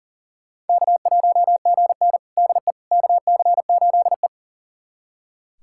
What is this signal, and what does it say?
Morse code, heard as a keyed tone, about 40 wpm, 700 Hz. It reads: K1GN DE KC8E
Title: USA Callsign DE Callsign - 4 Characters - 40wpm